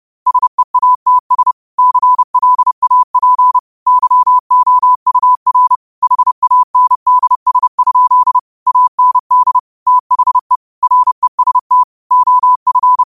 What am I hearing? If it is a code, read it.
Morse code, heard as a keyed tone, about 30 wpm, 1000 Hz. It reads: SEATS CLAP YOUR HANDS? AND THE REST OF